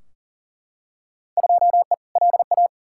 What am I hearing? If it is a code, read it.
Morse code, heard as a keyed tone, about 40 wpm, 700 Hz. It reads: 2E LA
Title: Field Day Exchanges Encore - 40wpm